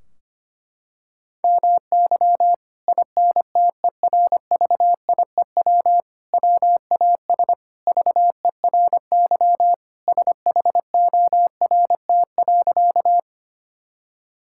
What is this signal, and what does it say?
Morse code, heard as a keyed tone, about 25 wpm, 700 Hz. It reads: MY INTERVIEW WAS VERY SHORT.
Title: Sentences from Top 1600 Words 25wpm